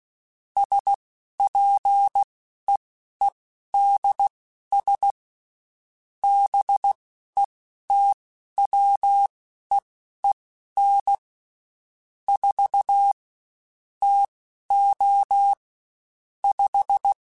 Morse code, heard as a keyed tone, about 16 wpm, spaced as for 11 wpm, 800 Hz. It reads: SPEEDS BETWEEN 4 TO 5
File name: Spaced Characters / Spaced Characters-mp3s / Weather4 21 sp @15